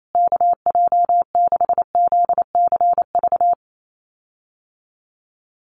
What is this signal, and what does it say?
Morse code, heard as a keyed tone, about 28 wpm, 700 Hz. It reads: KJ6ZCV
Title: USA Call Signs Repeated 3x 28wpm